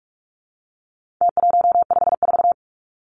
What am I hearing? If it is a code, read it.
Morse code, heard as a keyed tone, about 44 wpm, 700 Hz. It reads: T154
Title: Numbers - Contest Style - 4-Digit 45wpm